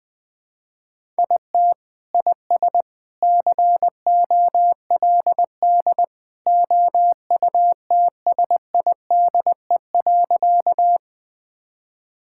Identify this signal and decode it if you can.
Morse code, heard as a keyed tone, about 21 wpm, 700 Hz. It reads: IT IS COLD OUTSIDE.
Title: Sentences from Top 1000 Words - Review Part 6 - 20wpm